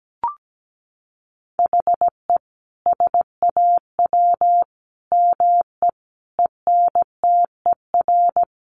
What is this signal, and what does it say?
Morse code, heard as a keyed tone, about 17 wpm, 700 Hz. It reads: HE SAW ME ENTER
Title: Sentences from Top 1300 Words 17wpm